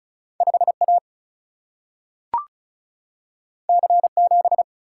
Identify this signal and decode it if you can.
Morse code, heard as a keyed tone, about 36 wpm, 700 Hz. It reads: 5A C7
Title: Single-Letter Single-Number 35wpm